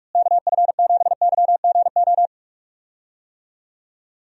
Morse code, heard as a keyed tone, about 42 wpm, 700 Hz. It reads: KF7YGO
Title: USA Call Signs - 6 Characters - Repeated 2x - 45wpm